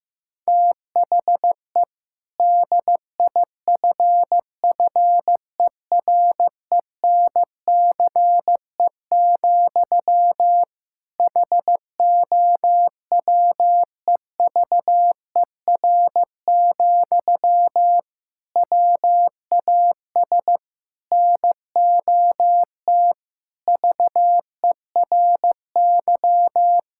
Morse code, heard as a keyed tone, about 16 wpm, 700 Hz. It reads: THE DIFFERENCE, HOWEVER, WAS NOT VERY
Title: Sentences from Top 1000 Words - Review Part 5 - 15wpm